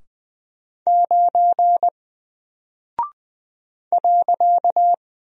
Morse code, heard as a keyed tone, about 20 wpm, 700 Hz. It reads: 9 .
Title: Single Character (Letter:Number:Punctuation:Prosign) Alt 20wpm